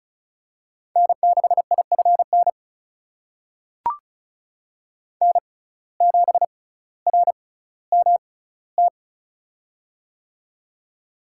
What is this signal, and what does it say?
Morse code, heard as a keyed tone, about 36 wpm, spaced as for 18 wpm, 700 Hz. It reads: N6IFN N7RMT
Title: USA Call Signs - 5 Characters - 6x Character Spacing - 35wpm